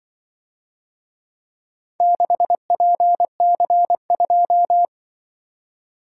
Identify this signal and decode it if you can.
Morse code, heard as a keyed tone, about 23 wpm, 700 Hz. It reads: 6PC2
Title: Speed-Racing - Four Letter-Number Combinations - 20wpm